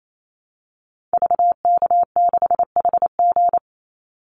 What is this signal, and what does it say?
Morse code, heard as a keyed tone, about 28 wpm, 700 Hz. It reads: VK6HG
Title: Call Signs Repeated 3x - 28wpm